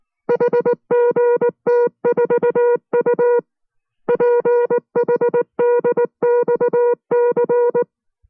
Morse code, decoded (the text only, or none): HGT4U PHDXC